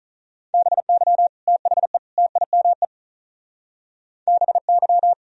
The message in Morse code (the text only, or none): BY THE TIME BY